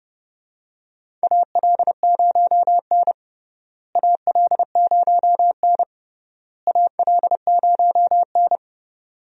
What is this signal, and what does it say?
Morse code, heard as a keyed tone, about 32 wpm, 700 Hz. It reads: AL0N AL0N AL0N
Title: USA Call Signs - 4 Characters - Repeated 3x - 30wpm